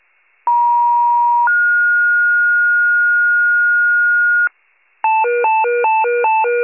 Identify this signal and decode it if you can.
Two-tone page, heard as a one-second tone, then a three-second tone, about 950 then 1430 Hz, with noise. It is heard from 0.5 to 4.5 s.
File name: pager / 76 FIRE / wcalert